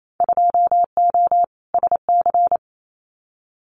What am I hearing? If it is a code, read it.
Morse code, heard as a keyed tone, about 28 wpm, 700 Hz. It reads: 2O SC
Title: Winter Field Day Exchanges 28wpm